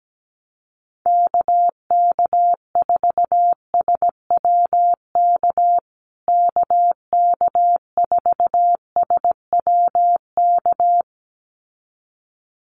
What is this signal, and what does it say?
Morse code, heard as a keyed tone, about 17 wpm, 700 Hz. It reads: KK4SWK KK4SWK
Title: USA Call Signs Repeated 2x 17wpm